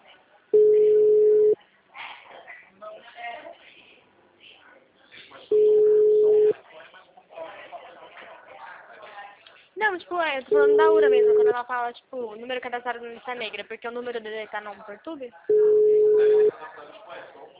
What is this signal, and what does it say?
Ringback tone (Europe, 425 Hz), with speech.
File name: out- -2010- - - .29877